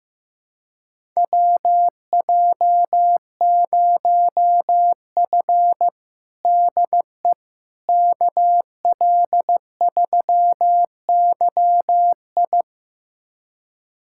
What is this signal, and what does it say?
Morse code, heard as a keyed tone, about 15 wpm, 700 Hz. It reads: WJ0F DE KL3YI
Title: USA Callsign DE Callsign 15wpm